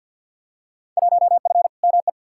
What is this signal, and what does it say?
Morse code, heard as a keyed tone, about 52 wpm, 700 Hz. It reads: 1F ME
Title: Field Day Exchanges Encore - 50wpm